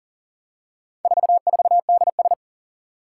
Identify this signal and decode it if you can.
Morse code, heard as a keyed tone, about 40 wpm, 700 Hz. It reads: 44DS
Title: Four Challenging Letter-Number Combinations 40wpm